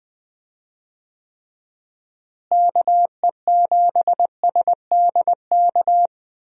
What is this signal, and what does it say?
Morse code, heard as a keyed tone, about 20 wpm, 700 Hz. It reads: KE7SDK
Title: USA Call Signs - 6 Characters - Repeated 2x - 20wpm